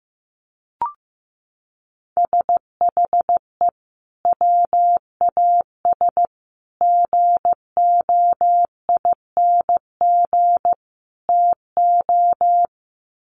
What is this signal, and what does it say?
Morse code, heard as a keyed tone, about 15 wpm, 700 Hz. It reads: SHE WAS GOING TO
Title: Sets of 4 Words 15wpm